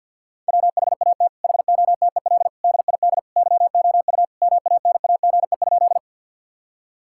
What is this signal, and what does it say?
Morse code, heard as a keyed tone, about 50 wpm, 700 Hz. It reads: WHAT HOTEL DID YOU MANAGE?
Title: Sentences from Top 1600 Words 50wpm